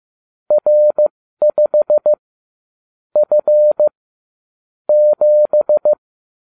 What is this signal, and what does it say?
Morse code, heard as a keyed tone, about 15 wpm, spaced as for 13 wpm, 600 Hz. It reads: R5 F 7